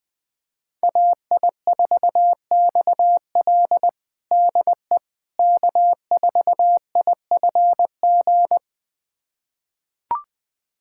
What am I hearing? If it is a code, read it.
Morse code, heard as a keyed tone, about 20 wpm, 700 Hz. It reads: AI4XL DE K4IFG
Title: USA Callsign DE Callsign - 5 Characters - 20wpm